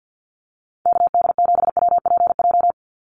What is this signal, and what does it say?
Morse code, heard as a keyed tone, about 50 wpm, 700 Hz. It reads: KD7WPJ
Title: NA SOTA Activator Callsigns 50wpm